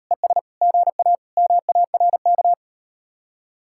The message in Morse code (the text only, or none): ES GA MARK